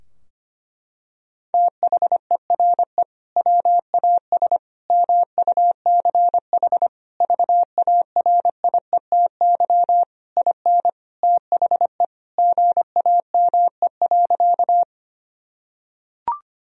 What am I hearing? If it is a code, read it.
Morse code, heard as a keyed tone, about 25 wpm, 700 Hz. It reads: THERE WAS MUCH VARIETY IN THE GAME.
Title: Sentences from Top 1000 Words - Review Part 4 - 25wpm